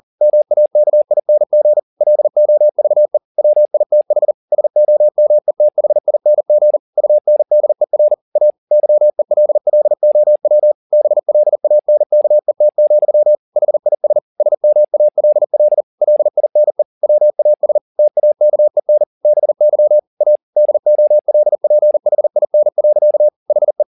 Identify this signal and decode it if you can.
Morse code, heard as a keyed tone, about 40 wpm, 600 Hz. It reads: MAKING LOVE WITH SOMETHING UNDER A YELLOW BLANKET, HIS SMALL LINE WAS TAKEN BY A DOLPHIN. HE
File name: the old man and the sea 2 40wpm